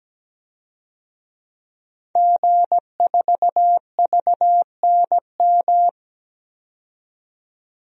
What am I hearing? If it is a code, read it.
Morse code, heard as a keyed tone, about 17 wpm, 700 Hz. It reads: G4VNM
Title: DXCC-100 Call Signs - 5 Characters - 17wpm